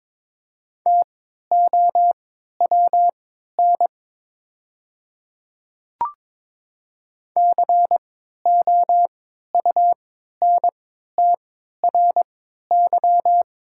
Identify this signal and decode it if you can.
Morse code, heard as a keyed tone, about 22 wpm, spaced as for 12 wpm, 700 Hz. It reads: TOWN COUNTRY
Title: 3x Character Spacing - Top 500 Words - 22wpm